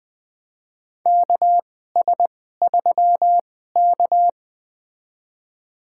Morse code, heard as a keyed tone, about 20 wpm, spaced as for 14 wpm, 700 Hz. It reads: KS3K
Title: USA Call Signs - 2x Character Spacing - 20wpm